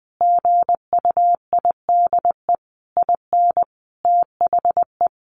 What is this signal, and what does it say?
Morse code, heard as a keyed tone, about 20 wpm, 700 Hz. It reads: GUIDE IN THE
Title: A Sound Of Thunder 20wpm